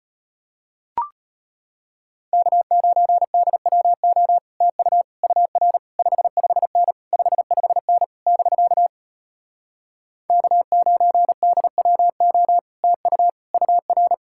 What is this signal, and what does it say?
Morse code, heard as a keyed tone, about 33 wpm, 700 Hz. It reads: K9DWO TU UR 55N 55N <BK> K9DWO TU UR